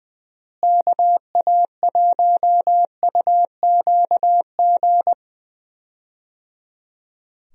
Morse code, heard as a keyed tone, about 20 wpm, 700 Hz. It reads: KA1UQG
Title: USA Call Signs - 6 Characters - Repeated 3x - 20wpm